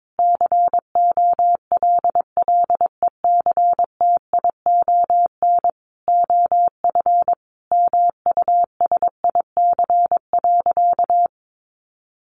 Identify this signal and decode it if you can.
Morse code, heard as a keyed tone, about 22 wpm, 700 Hz. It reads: COLLECTION OF MUSIC.
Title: Sentences from Top 1200 Words 22wpm